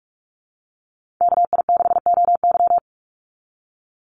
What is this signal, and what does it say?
Morse code, heard as a keyed tone, about 44 wpm, 700 Hz. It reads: KI6OY
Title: Call Signs - 45wpm